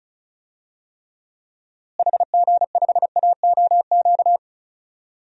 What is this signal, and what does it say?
Morse code, heard as a keyed tone, about 36 wpm, 700 Hz. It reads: HG5AOQ
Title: DXCC-100 Call Signs - 6 Characters - Repeated 3x - 35wpm